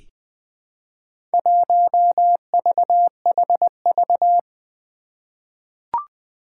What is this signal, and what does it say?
Morse code, heard as a keyed tone, about 20 wpm, 700 Hz. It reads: 1VHV
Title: Four Challenging Letter-Number Combinations 20wpm